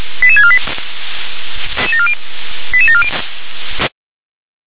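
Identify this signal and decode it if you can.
Five-tone selective call (ZVEI), heard as quick runs of five tones, tones of about 70 ms. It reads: 88648 88740
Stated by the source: Received on an SDR as a ZVEI five-tone call, code 88154.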